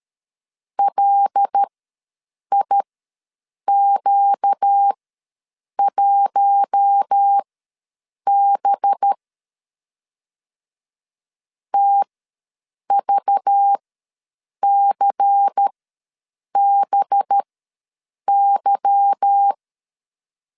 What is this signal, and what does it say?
Morse code, heard as a keyed tone, about 13 wpm, spaced as for 7 wpm, 800 Hz. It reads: LIQ1B TVCBY
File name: LEZIONI / 8 YJ45 / 8 Y ws 12wpm 3409